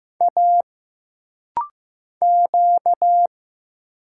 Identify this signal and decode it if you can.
Morse code, heard as a keyed tone, about 15 wpm, 700 Hz. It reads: A Q
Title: Single Letters - Rapid-Fire 15wpm